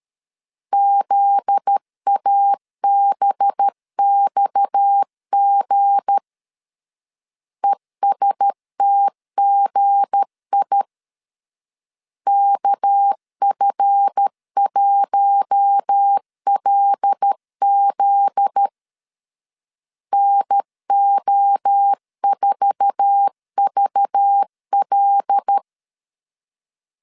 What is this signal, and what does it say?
Morse code, heard as a keyed tone, about 13 wpm, 800 Hz. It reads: ZABXG ESTGI KF1LZ NO4VL
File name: CW / DET / 1 9 1500C S C 12wpm 3811